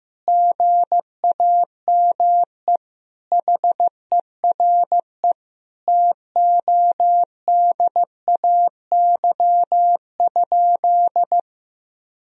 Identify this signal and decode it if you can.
Morse code, heard as a keyed tone, about 15 wpm, 700 Hz. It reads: GAME HERE TODAY?
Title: Sentences from Top 1000 Words - Review Part 3 - 15wpm